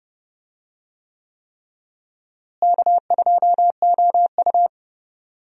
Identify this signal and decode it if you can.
Morse code, heard as a keyed tone, about 30 wpm, 700 Hz. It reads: K2OU
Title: USA Call Signs - 4 Characters - Repeated 3x - 30wpm